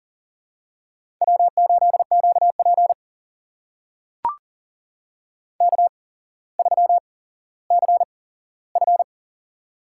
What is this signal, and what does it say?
Morse code, heard as a keyed tone, about 40 wpm, spaced as for 17 wpm, 700 Hz. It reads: W8QP K3CF
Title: USA Call Signs - 4 Characters - 8x Character Spacing - 40wpm